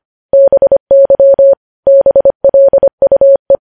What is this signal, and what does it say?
Morse code, heard as a keyed tone, about 25 wpm, 550 Hz. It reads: BY BLUE
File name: podcast / News headlines-1x-25x25